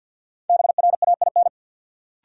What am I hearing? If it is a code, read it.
Morse code, heard as a keyed tone, about 53 wpm, 700 Hz. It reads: BRAIN
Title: Sentences from Top 1000 Words - Review Part 6 - New Word Drill - 50wpm